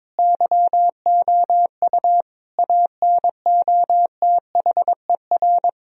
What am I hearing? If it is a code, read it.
Morse code, heard as a keyed tone, about 22 wpm, 700 Hz. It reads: YOU ANOTHER
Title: Sentences from Top 1000 Words - Review Part 1 - 22wpm